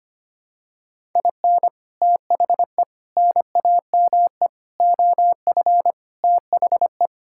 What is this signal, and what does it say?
Morse code, heard as a keyed tone, about 25 wpm, 700 Hz. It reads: IN THE NAME OF THE
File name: Sets of 9 Words - 25wpm